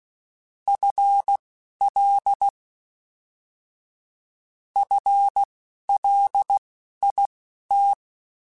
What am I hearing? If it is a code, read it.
Morse code, heard as a keyed tone, about 16 wpm, spaced as for 11 wpm, 800 Hz. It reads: FL FLIT
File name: FL / FL mp3s / FL Prefixes 3x2 @10